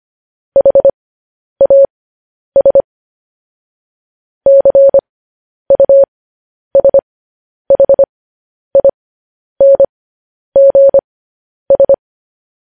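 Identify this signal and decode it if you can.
Morse code, heard as a keyed tone, about 25 wpm, spaced as for 10 wpm, 550 Hz. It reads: HAS CUSHINGS
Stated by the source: Morse code news headlines practice, at 10 WPM.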